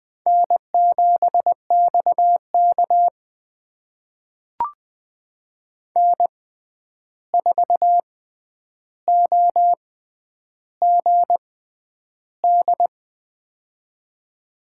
Morse code, heard as a keyed tone, about 20 wpm, spaced as for 10 wpm, 700 Hz. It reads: N7XK N4OGD
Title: USA Call Signs - 6x Character Spacing - 20wpm